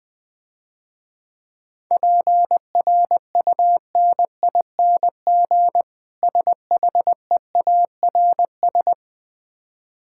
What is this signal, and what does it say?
Morse code, heard as a keyed tone, about 20 wpm, 700 Hz. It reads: PRUNING SHEARS
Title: Easy Words - Speed-Racing with Hints - 20wpm